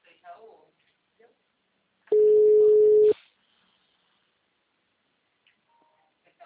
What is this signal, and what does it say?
Ringback tone (Europe, 425 Hz), with speech.
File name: out- -2007- - - .789